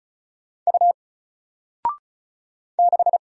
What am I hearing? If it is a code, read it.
Morse code, heard as a keyed tone, about 34 wpm, 700 Hz. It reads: U 6
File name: Single-Letter-Number-35wpm